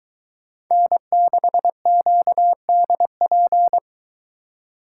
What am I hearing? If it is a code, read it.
Morse code, heard as a keyed tone, about 22 wpm, 700 Hz. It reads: N6QDP